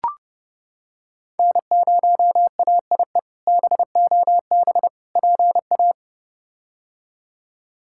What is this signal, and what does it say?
Morse code, heard as a keyed tone, about 30 wpm, 700 Hz. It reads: N0AIE BOB PA
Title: K1USN SST QSO Practice - 30wpm